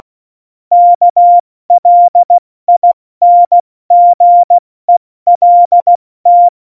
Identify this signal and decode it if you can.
Morse code, heard as a keyed tone, about 16 wpm, 700 Hz. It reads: KLINGELT
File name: Morsekurs / Witze / Witz 02. Oktober 2018, 70 F80